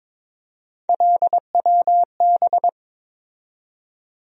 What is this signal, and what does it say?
Morse code, heard as a keyed tone, about 22 wpm, 700 Hz. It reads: LWB
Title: USA Call Sign Postfixes - 3 Letters 22wpm